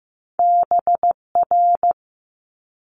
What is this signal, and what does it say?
Morse code, heard as a keyed tone, about 15 wpm, 700 Hz. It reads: BR
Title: Multi-letter Phonemes Quick Fire 15wpm